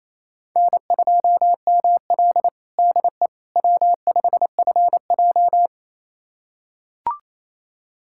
Morse code, heard as a keyed tone, about 28 wpm, 700 Hz. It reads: N2ML DE W5FJ
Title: USA Callsign DE Callsign - 4 Characters - 28wpm